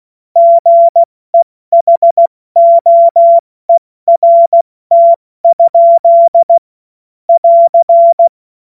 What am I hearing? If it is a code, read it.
Morse code, heard as a keyed tone, about 16 wpm, 680 Hz. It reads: GEHOERT? <AR>
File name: Morsekurs / Witze / Witz 17 Juni Mai 2025 70 F80